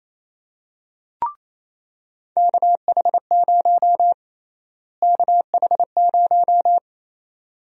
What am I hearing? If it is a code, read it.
Morse code, heard as a keyed tone, about 28 wpm, 700 Hz. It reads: KH0 KH0
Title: USA Call Sign Prefixes - Repeated 2x - 28wpm